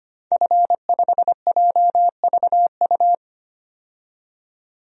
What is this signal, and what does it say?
Morse code, heard as a keyed tone, about 25 wpm, 700 Hz. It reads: F5JVU
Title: DXCC-100 Call Signs - 5 Characters - 25wpm